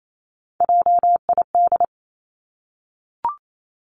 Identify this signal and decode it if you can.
Morse code, heard as a keyed tone, about 28 wpm, 700 Hz. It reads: JID
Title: Three Challenging Letter-Number Combinations 28wpm